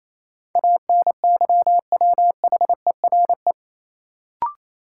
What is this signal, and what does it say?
Morse code, heard as a keyed tone, about 28 wpm, 700 Hz. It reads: ANYWHERE
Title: Sentences from Top 1400 Words - New Word Drill - 28wpm